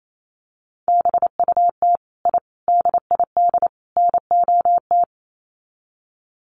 Morse code, heard as a keyed tone, about 28 wpm, 700 Hz. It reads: BUT I DID NOT
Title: Sets of 4 Words 28wpm